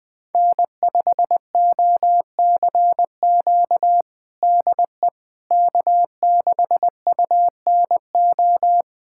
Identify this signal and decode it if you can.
Morse code, heard as a keyed tone, about 21 wpm, 700 Hz. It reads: N5OCQ DE K6UNO
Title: USA Callsign DE Callsign - 5 Characters - 20wpm